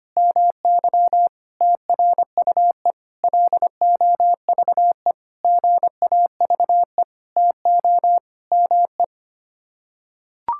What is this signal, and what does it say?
Morse code, heard as a keyed tone, about 25 wpm, 700 Hz. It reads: MY TRUE LOVE GAVE TO ME
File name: Sets of 9 Words - 25wpm